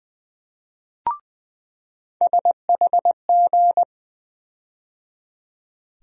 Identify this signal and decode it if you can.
Morse code, heard as a keyed tone, about 20 wpm, 700 Hz. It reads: SHG
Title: USA Call Sign Postfixes - 3 Letters 20wpm